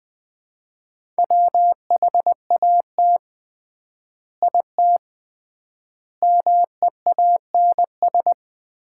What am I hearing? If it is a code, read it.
Morse code, heard as a keyed tone, about 20 wpm, 700 Hz. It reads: WHAT IT MEANS
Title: 3x Word-Spacing - Sets of 5 Words - 20wpm